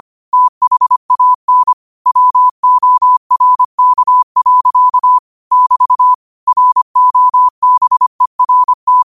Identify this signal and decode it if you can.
Morse code, heard as a keyed tone, about 25 wpm, 1000 Hz. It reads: THAN WORK. <BT> ROBERT
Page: New quotes every day in morse code at 25 Words per minute.